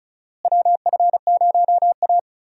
Morse code, heard as a keyed tone, about 34 wpm, 700 Hz. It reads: WF0A
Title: USA Call Signs - 4 Characters - Repeated 2x - 35wpm